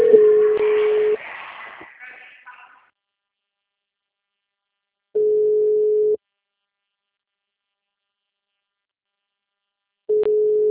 Ringback tone (Europe, 425 Hz).